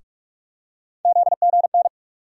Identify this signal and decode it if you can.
Morse code, heard as a keyed tone, about 46 wpm, 700 Hz. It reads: ZGN
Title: USA Call Sign Postfixes - 3 Letters 45wpm